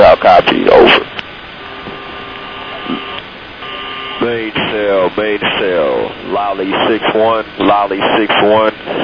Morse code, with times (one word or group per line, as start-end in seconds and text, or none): none